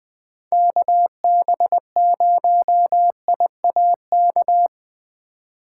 Morse code, heard as a keyed tone, about 20 wpm, 700 Hz. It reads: KB0IAK
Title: USA Call Signs - 6 Characters - 2x Character Spacing - 20wpm